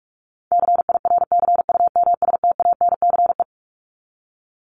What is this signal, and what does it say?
Morse code, heard as a keyed tone, about 44 wpm, 700 Hz. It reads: CIRCUMSTANCE
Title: ES:S Suffix - Top 100 Words with Base Words 45wpm